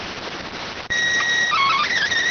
SSTV (slow-scan television), PD 290.